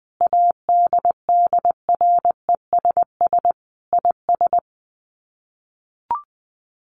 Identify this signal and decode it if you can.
Morse code, heard as a keyed tone, about 20 wpm, 700 Hz. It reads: ADDRESS IS
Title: Sets of 5 Words Encore2 20wpm